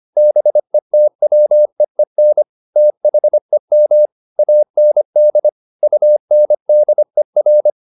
Morse code, heard as a keyed tone, about 25 wpm, 600 Hz. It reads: BETWEEN THEM AND UNDER
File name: the old man and the sea part 1 25wpm